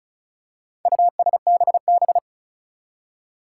Morse code, heard as a keyed tone, about 34 wpm, 700 Hz. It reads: USBB
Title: Four Challenging Letter-Number Combinations 35wpm